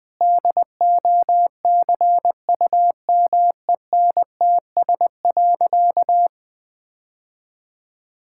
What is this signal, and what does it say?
Morse code, heard as a keyed tone, about 20 wpm, 700 Hz. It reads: DOCUMENTS.
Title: Easy Sentences with Hints - 20wpm